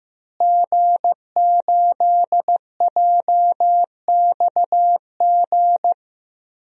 Morse code, heard as a keyed tone, about 16 wpm, 700 Hz. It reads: G8JXG
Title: DXCC-100 Call Signs - 5 Characters - Repeated 2x - 15wpm